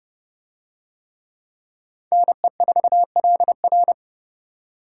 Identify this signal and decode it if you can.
Morse code, heard as a keyed tone, about 29 wpm, 700 Hz. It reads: NE4LR